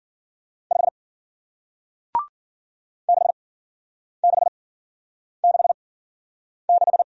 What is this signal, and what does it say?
Morse code, heard as a keyed tone, about 48 wpm, spaced as for 9 wpm, 700 Hz. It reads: 5 6666
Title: Speed-Racing Single Letter-Number 40wpm